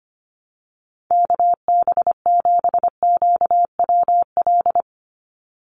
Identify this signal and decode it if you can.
Morse code, heard as a keyed tone, about 25 wpm, 700 Hz. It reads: KB7QWL
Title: USA Call Signs 25wpm